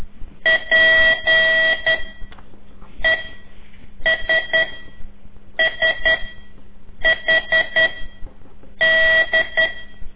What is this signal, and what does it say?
Morse code, heard as a keyed tone, about 11 wpm, 1950 Hz. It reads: P E S S H D
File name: CORSI CW